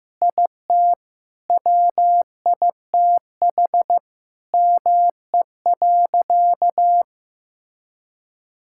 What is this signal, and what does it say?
Morse code, heard as a keyed tone, about 15 wpm, 700 Hz. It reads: IT WITH ME.
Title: Sentences from Top 1000 Words - Review Part 6 - 15wpm